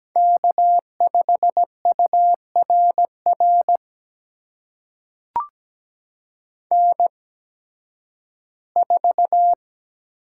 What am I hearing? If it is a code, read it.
Morse code, heard as a keyed tone, about 17 wpm, spaced as for 11 wpm, 700 Hz. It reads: K5URR N4
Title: USA Call Signs - 5 Characters - 8x Character Spacing - 17wpm